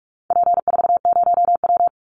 Morse code, heard as a keyed tone, about 44 wpm, 700 Hz. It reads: P40W
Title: Call Signs Repeated 2x - 45wpm